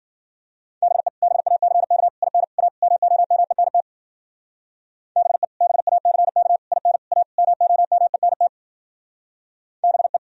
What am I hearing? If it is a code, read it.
Morse code, heard as a keyed tone, about 51 wpm, 700 Hz. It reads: BE BACK IN A MOMENT BE BACK IN A MOMENT BE